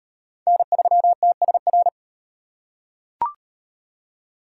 Morse code, heard as a keyed tone, about 36 wpm, 700 Hz. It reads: N3TSR